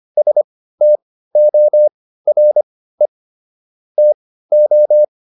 Morse code, heard as a keyed tone, about 25 wpm, spaced as for 15 wpm, 600 Hz. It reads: STORE TO